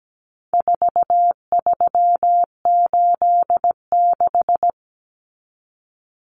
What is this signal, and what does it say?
Morse code, heard as a keyed tone, about 17 wpm, 700 Hz. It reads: 4386
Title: Numbers - 4-Digit 17wpm